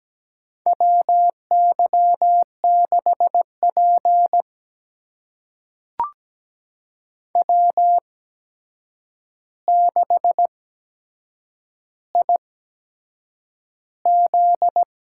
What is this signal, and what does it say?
Morse code, heard as a keyed tone, about 17 wpm, spaced as for 7 wpm, 700 Hz. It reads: WY6P W6IZ
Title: USA Call Signs - 4 Characters - 8x Character Spacing - 17wpm